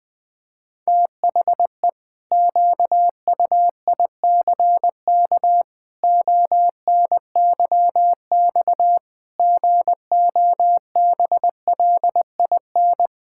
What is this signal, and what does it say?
Morse code, heard as a keyed tone, about 20 wpm, 700 Hz. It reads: THE QUICK ONYX GOBLIN
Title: Pangram Challenge v2 - 20wpm